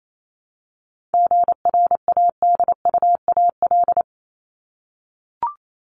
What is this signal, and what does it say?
Morse code, heard as a keyed tone, about 28 wpm, 700 Hz. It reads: GRADUAL
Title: LY Suffix - Top 100 Words with Base Words 28wpm